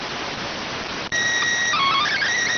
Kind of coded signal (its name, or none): SSTV (slow-scan television)